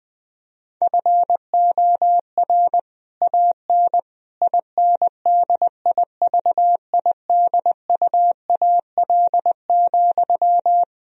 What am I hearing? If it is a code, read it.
Morse code, heard as a keyed tone, about 21 wpm, 700 Hz. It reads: FOR AN INDIVIDUAL,
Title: Sentences from Top 1000 Words - Review Part 4 - 20wpm